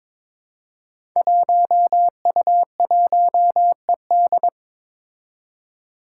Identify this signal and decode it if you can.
Morse code, heard as a keyed tone, about 22 wpm, 700 Hz. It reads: 1U1ED
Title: Five Challenging Letter-Number Combinations 22wpm